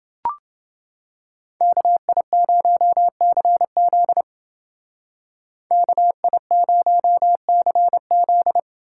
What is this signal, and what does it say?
Morse code, heard as a keyed tone, about 28 wpm, 700 Hz. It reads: KI0CZ KI0CZ